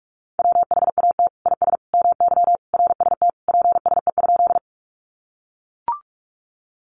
Morse code, heard as a keyed tone, about 44 wpm, 700 Hz. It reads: WHAT IS MY RST PSE?
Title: Common QSO Phrases 45wpm